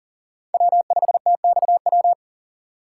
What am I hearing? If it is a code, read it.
Morse code, heard as a keyed tone, about 37 wpm, 700 Hz. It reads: W5TXW
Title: USA Call Signs - 5 Characters - 3x Character Spacing - 40wpm